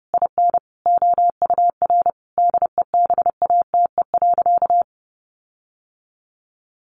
Morse code, heard as a keyed tone, about 28 wpm, 700 Hz. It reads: IN OUR DEBATE.
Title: Sentences from Top 1500 Words 30wpm